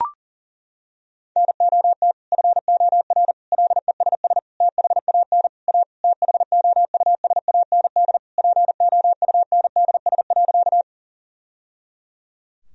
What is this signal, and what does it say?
Morse code, heard as a keyed tone, about 40 wpm, 700 Hz. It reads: NOT FOR LESS THAN A THOUSAND POUNDS.
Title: Sentences from Top 1000 Words - Review Part 3 - 40wpm